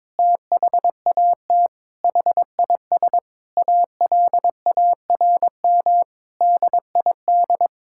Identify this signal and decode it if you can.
Morse code, heard as a keyed tone, about 22 wpm, 700 Hz. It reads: THAT HIS ALARM DID
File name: practice / 22wpm / Sentences with 4-Word Phrases - 22wpm